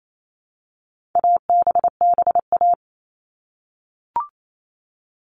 Morse code, heard as a keyed tone, about 28 wpm, 700 Hz. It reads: ABBA
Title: All Cut Numbers - 4-Digit 28wpm